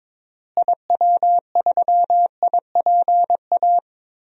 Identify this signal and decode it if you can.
Morse code, heard as a keyed tone, about 22 wpm, 700 Hz. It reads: IW3IPA
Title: DXCC-100 Call Signs - 6 Characters - Repeated 3x - 22wpm